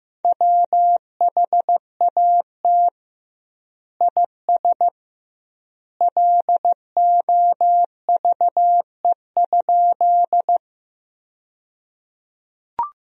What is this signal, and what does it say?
Morse code, heard as a keyed tone, about 15 wpm, 700 Hz. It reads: WHAT IS LOVE?
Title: 2x Word-Spacing - Sentences from Top 400 Words - 15wpm